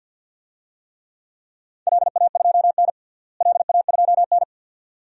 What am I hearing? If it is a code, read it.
Morse code, heard as a keyed tone, about 51 wpm, 700 Hz. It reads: LA2N LA2N
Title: DXCC-100 Call Signs - 4 Characters - Repeated 2x - 50wpm